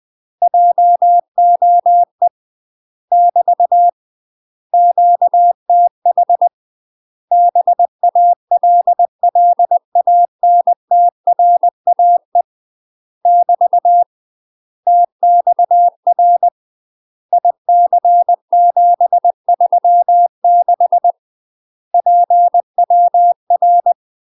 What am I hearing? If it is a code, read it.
Morse code, heard as a keyed tone, about 20 wpm, 700 Hz. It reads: JOE <BT> QTH BALLANTRAE <BT> TXR IC736 PWR
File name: UK MASTER QSOs TRAINING / 20WPM / UK MASTER QSOs 0017